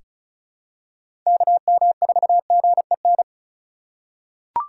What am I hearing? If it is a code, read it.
Morse code, heard as a keyed tone, about 36 wpm, 700 Hz. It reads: KM4GEN
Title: USA Call Signs - 6 Characters - 4x Character Spacing - 35wpm